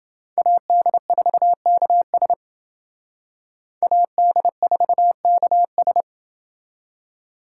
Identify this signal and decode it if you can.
Morse code, heard as a keyed tone, about 28 wpm, 700 Hz. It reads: AD4KS AD4KS